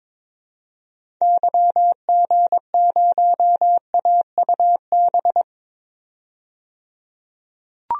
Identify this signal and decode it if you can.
Morse code, heard as a keyed tone, about 21 wpm, 700 Hz. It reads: YG0AUB
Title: DXCC-100 Call Signs - 6 Characters - 22wpm